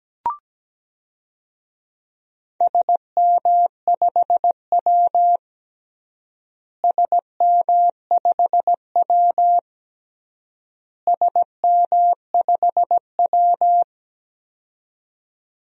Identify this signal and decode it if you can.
Morse code, heard as a keyed tone, about 17 wpm, 700 Hz. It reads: SM5W SM5W SM5W
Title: DXCC-100 Call Signs - 4 Characters - Repeated 3x - 17wpm